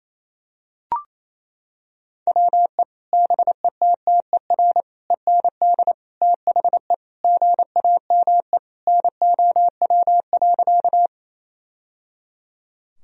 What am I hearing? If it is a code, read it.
Morse code, heard as a keyed tone, about 28 wpm, 700 Hz. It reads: WE BETTER END THE GAME NOW.
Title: Sentences from Top 1000 Words - Review Part 3 - 28wpm